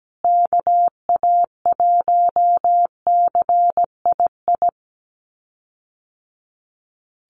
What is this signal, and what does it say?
Morse code, heard as a keyed tone, about 17 wpm, 700 Hz. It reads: KA1CII
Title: USA Call Signs Repeated 3x 17wpm